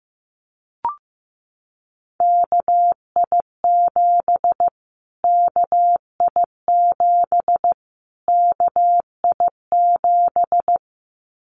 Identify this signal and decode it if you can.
Morse code, heard as a keyed tone, about 15 wpm, 700 Hz. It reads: KI7 KI7 KI7
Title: Callsign Prefixes for Top 25 DXCC - Repeated 3x - 15wpm